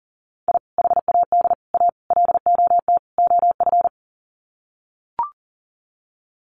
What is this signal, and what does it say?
Morse code, heard as a keyed tone, about 40 wpm, 700 Hz. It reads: I HAD A LOT OF
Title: Sets of 5 Words Encore 40wpm